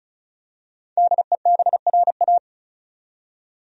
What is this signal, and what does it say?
Morse code, heard as a keyed tone, about 36 wpm, 700 Hz. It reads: DEBRA
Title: USA Common Names - 35wpm